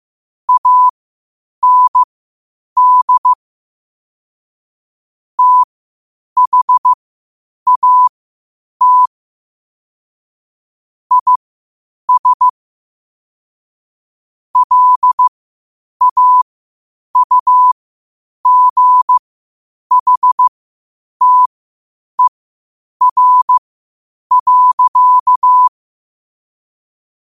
Morse code, heard as a keyed tone, about 15 wpm, spaced as for 8 wpm, 1000 Hz. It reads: AND THAT IS LAUGHTER.
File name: qotdfiles / 8wpm